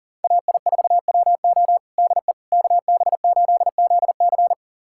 Morse code, heard as a keyed tone, about 40 wpm, 700 Hz. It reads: AI4WO DE KB8ZC
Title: USA Callsign DE Callsign - 5 Characters - 40wpm